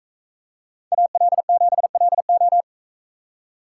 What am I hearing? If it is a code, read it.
Morse code, heard as a keyed tone, about 40 wpm, 700 Hz. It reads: AL7LO